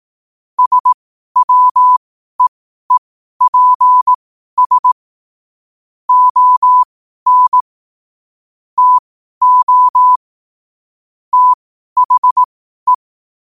Morse code, heard as a keyed tone, about 18 wpm, spaced as for 12 wpm, 1000 Hz. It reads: SWEEPS ON TO THE
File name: qotdfiles / 12wpm